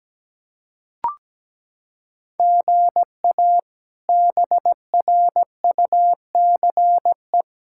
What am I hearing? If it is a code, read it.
Morse code, heard as a keyed tone, about 17 wpm, 700 Hz. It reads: GA BRUCE
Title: K1USN SST QSO Practice - 17wpm